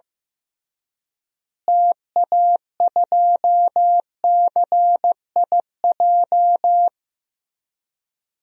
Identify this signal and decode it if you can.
Morse code, heard as a keyed tone, about 16 wpm, 700 Hz. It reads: TA2CIJ
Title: DXCC-100 Call Signs - 6 Characters - Repeated 3x - 15wpm